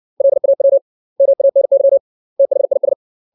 Morse code, heard as a keyed tone, about 60 wpm, 550 Hz. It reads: LAW MANY THIS
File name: 632 words / 632 Words 60WPM